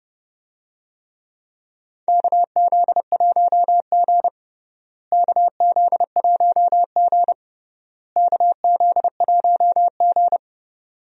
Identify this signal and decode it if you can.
Morse code, heard as a keyed tone, about 30 wpm, 700 Hz. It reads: KZ1G KZ1G KZ1G
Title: USA Call Signs - 4 Characters - Repeated 3x - 30wpm